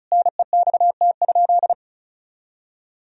Morse code, heard as a keyed tone, about 36 wpm, 700 Hz. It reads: NEXT?
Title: Sentences from Top 1000 Words - Review Part 2 - 35wpm